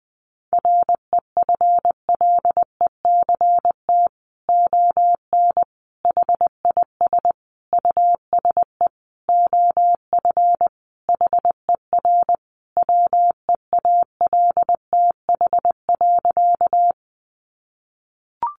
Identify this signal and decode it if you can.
Morse code, heard as a keyed tone, about 20 wpm, 700 Hz. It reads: REFLECT ON HIS USE OF HER WEALTH.
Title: Sentences from Top 1500 Words 20wpm